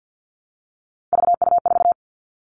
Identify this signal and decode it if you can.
Morse code, heard as a keyed tone, about 50 wpm, 700 Hz. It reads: VU4
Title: All Cut Numbers - 3-Digit 50wpm